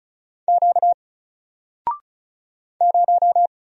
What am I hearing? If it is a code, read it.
Morse code, heard as a keyed tone, about 36 wpm, 700 Hz. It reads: Q 0
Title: Single Character (Letter:Number:Punctuation:Prosign) 35wpm